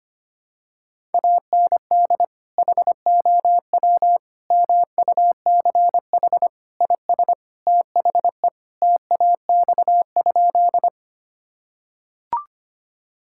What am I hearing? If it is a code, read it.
Morse code, heard as a keyed tone, about 25 wpm, 700 Hz. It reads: AND HOW MUCH IS THE TAX?
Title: Sentences from Top 1000 Words - Review Part 5 - 25wpm